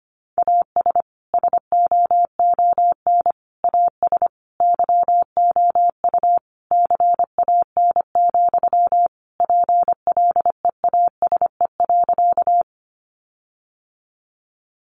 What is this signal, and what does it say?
Morse code, heard as a keyed tone, about 25 wpm, 700 Hz. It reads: AS SOON AS YOU CAN, PLEASE.
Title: Sentences from Top 1500 Words 25wpm